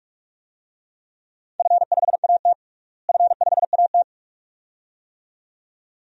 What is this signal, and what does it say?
Morse code, heard as a keyed tone, about 46 wpm, 700 Hz. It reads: F5AT F5AT
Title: DXCC-100 Call Signs - 4 Characters - Repeated 2x - 45wpm